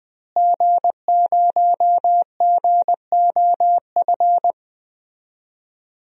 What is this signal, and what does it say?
Morse code, heard as a keyed tone, about 20 wpm, 700 Hz. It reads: G0GOF
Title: DXCC-100 Call Signs - 5 Characters - Repeated 2x - 20wpm